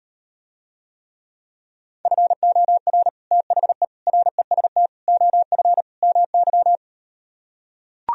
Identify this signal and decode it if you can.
Morse code, heard as a keyed tone, about 38 wpm, 700 Hz. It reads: FOR THE REST OF MY